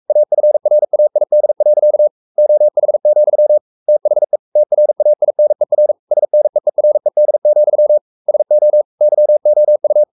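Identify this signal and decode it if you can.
Morse code, heard as a keyed tone, about 44 wpm, 600 Hz. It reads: AFRAID. OH, THE TRAINER SNEERED, SO YOU
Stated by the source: Morse code at 43 wpm sending Ken Follett's novel Never in English, from CW med Gnister.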